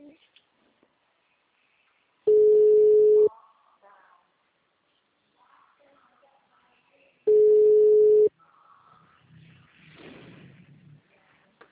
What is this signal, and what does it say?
Ringback tone (Europe, 425 Hz), with speech.
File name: out- -2002- - - .44